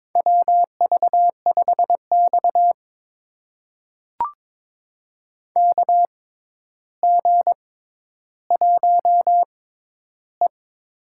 Morse code, heard as a keyed tone, about 22 wpm, spaced as for 11 wpm, 700 Hz. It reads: WV5X KG1E
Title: USA Call Signs - 4 Characters - 6x Character Spacing - 22wpm